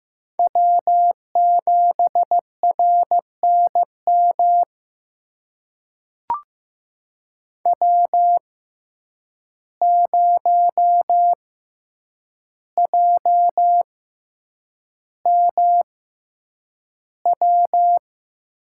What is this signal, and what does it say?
Morse code, heard as a keyed tone, about 15 wpm, spaced as for 8 wpm, 700 Hz. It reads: W7RNM W0JMW
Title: USA Call Signs - 5 Characters - 6x Character Spacing - 15wpm